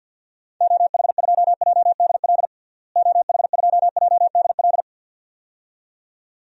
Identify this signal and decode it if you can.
Morse code, heard as a keyed tone, about 53 wpm, 700 Hz. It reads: OH2JDL OH2JDL
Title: DXCC-100 Call Signs - 6 Characters - Repeated 2x - 50wpm